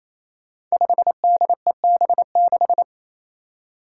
Morse code, heard as a keyed tone, about 28 wpm, 700 Hz. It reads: 5DEB6
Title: Five Challenging Letter-Number Combinations 28wpm